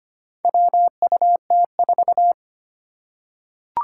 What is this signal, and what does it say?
Morse code, heard as a keyed tone, about 25 wpm, 700 Hz. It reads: WUT4
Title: Four Challenging Letter-Number Combinations 25wpm